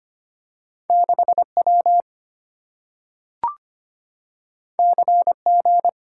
Morse code, heard as a keyed tone, about 25 wpm, 700 Hz. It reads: 6W CG
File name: Two Letter-Number Combinations - 25wpm